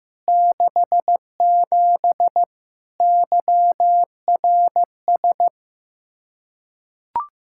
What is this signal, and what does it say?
Morse code, heard as a keyed tone, about 15 wpm, 700 Hz. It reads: 67 YRS
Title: Practice QSO - Exchange 3 - More Info - 15wpm